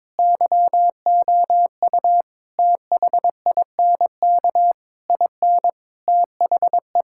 Morse code, heard as a keyed tone, about 21 wpm, 700 Hz. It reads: YOU THINK IN THE
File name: practice / 22wpm / Sentences with 8-Word Phrases - 22wpm